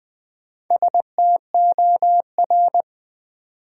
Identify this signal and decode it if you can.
Morse code, heard as a keyed tone, about 20 wpm, 700 Hz. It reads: STOR
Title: Build A Word - 1 Letter At A Time - 3 to 5-Letter Words - 20wpm